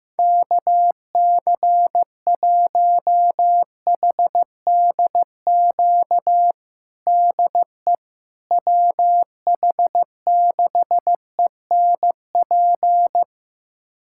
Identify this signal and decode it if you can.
Morse code, heard as a keyed tone, about 15 wpm, 700 Hz. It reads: KC1HDQ DE WH6ENP
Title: USA Callsign DE Callsign - 6 Characters - 15wpm